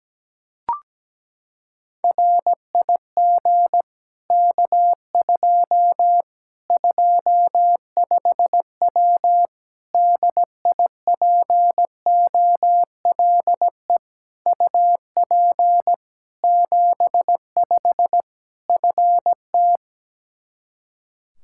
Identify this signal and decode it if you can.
Morse code, heard as a keyed tone, about 17 wpm, 700 Hz. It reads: RIG K2 25W DIPOLE UP 75 FT
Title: Practice QSO - Exchange 2 - Info - 17wpm